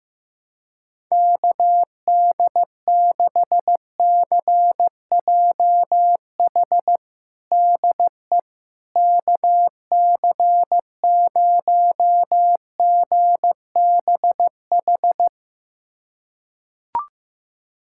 Morse code, heard as a keyed tone, about 16 wpm, 700 Hz. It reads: KD6CJH DE KC0GBH
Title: USA Callsign DE Callsign 15wpm